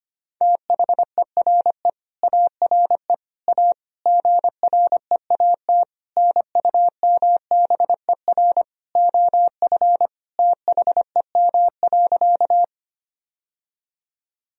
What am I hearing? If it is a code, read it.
Morse code, heard as a keyed tone, about 25 wpm, 700 Hz. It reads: THERE ARE A GREAT NUMBER OF THEM.
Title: Sentences from Top 1000 Words - Review Part 1 - 25wpm